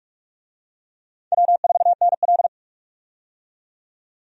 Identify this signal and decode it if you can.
Morse code, heard as a keyed tone, about 42 wpm, 700 Hz. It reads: W4NL
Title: USA Call Signs - 4 Characters - 45wpm